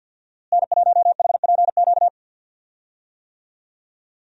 Morse code, heard as a keyed tone, about 53 wpm, 700 Hz. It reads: N1HPQ
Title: USA Call Signs - 5 Characters - Repeated 2x - 50wpm